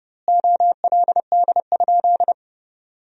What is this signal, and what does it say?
Morse code, heard as a keyed tone, about 30 wpm, 700 Hz. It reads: OLD?
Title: 2x Word-Spacing - Sentences from Top 400 Words - 30wpm